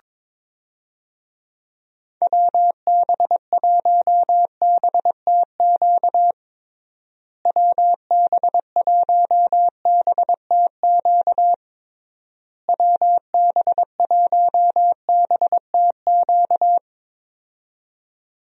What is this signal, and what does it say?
Morse code, heard as a keyed tone, about 21 wpm, 700 Hz. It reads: WB1BTQ WB1BTQ WB1BTQ
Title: USA Call Signs - 6 Characters - Repeated 3x - 22wpm